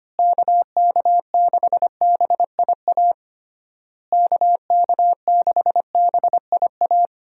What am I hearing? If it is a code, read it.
Morse code, heard as a keyed tone, about 25 wpm, 700 Hz. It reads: KK6BIA KK6BIA
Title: USA Call Signs - 6 Characters - Repeated 3x - 25wpm